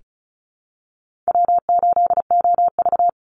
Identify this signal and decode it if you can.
Morse code, heard as a keyed tone, about 34 wpm, 700 Hz. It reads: W8OV
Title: Easy Call Signs Repeated 2x - 35wpm